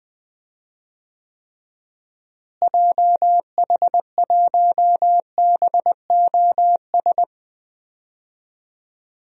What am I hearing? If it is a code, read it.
Morse code, heard as a keyed tone, about 20 wpm, 700 Hz. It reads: JH1BOS
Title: DXCC-100 Call Signs - 6 Characters - Repeated 2x - 20wpm